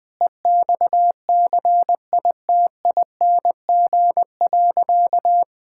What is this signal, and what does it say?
Morse code, heard as a keyed tone, about 21 wpm, 700 Hz. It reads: EXCITING.
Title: Sentences with 4-Word Phrases - 20wpm